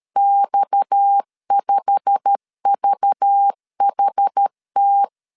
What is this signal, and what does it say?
Morse code, heard as a keyed tone, about 13 wpm, 800 Hz. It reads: X5VHT